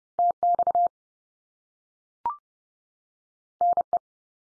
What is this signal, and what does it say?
Morse code, heard as a keyed tone, about 30 wpm, 700 Hz. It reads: TX NE
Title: US States 30WPM